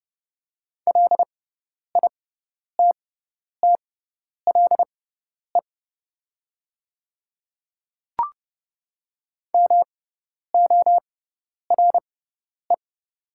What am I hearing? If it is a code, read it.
Morse code, heard as a keyed tone, about 30 wpm, spaced as for 10 wpm, 690 Hz. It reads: LITTLE MORE
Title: 6x Character Spacing - Top 300 Words - 30wpm